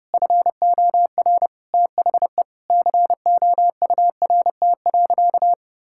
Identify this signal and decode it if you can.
Morse code, heard as a keyed tone, about 32 wpm, 700 Hz. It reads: FOR THE COURT.
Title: Sentences from Top 1600 Words 30wpm